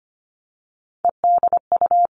Morse code, heard as a keyed tone, about 25 wpm, 700 Hz. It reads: EDU
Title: All Cut Numbers - 3-Digit 25wpm